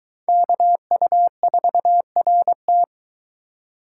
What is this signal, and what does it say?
Morse code, heard as a keyed tone, about 21 wpm, 700 Hz. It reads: KU4RT